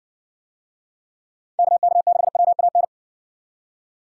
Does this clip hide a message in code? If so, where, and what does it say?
Yes: Morse code from 1.6 to 2.8 s, heard as a keyed tone, about 56 wpm, 700 Hz. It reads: KK6WAN